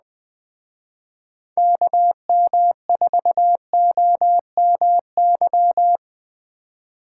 Morse code, heard as a keyed tone, about 20 wpm, 700 Hz. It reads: KM4OMY